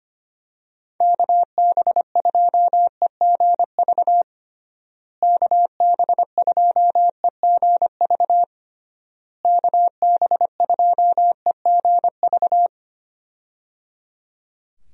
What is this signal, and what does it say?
Morse code, heard as a keyed tone, about 25 wpm, 700 Hz. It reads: KB2EGV KB2EGV KB2EGV
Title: USA Call Signs - 6 Characters - Repeated 3x - 25wpm